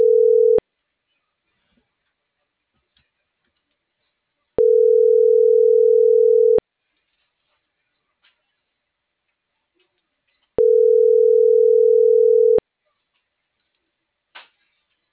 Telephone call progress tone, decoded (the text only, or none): ringback tone